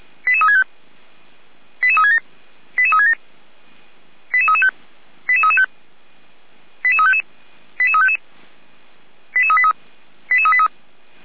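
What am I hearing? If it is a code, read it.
Five-tone selective call (ZVEI), heard as quick runs of five tones, tones of about 70 ms. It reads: 88365 88367 88367 88335 88335 88355 88355 88383 88383
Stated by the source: ZVEI five-tone selective call received with an SDR.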